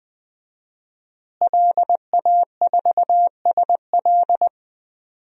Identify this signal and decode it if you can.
Morse code, heard as a keyed tone, about 20 wpm, 700 Hz. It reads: LA4SL
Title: DXCC-100 Call Signs - 5 Characters - Repeated 3x - 20wpm